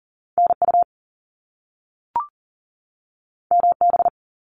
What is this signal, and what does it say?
Morse code, heard as a keyed tone, about 40 wpm, 700 Hz. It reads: NU MB
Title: Canadian Provinces and Territories 40wpm